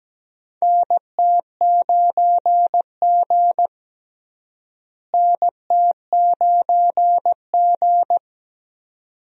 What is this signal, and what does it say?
Morse code, heard as a keyed tone, about 17 wpm, 700 Hz. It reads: NT9G NT9G
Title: USA Call Signs - 4 Characters - Repeated 2x - 17wpm